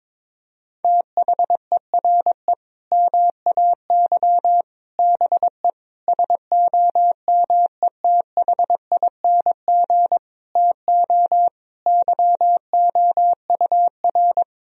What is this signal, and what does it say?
Morse code, heard as a keyed tone, about 22 wpm, 700 Hz. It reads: THERE MAY BE SOMETHING TO YOUR
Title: Sentences from Top 1000 Words - Review Part 3 - 22wpm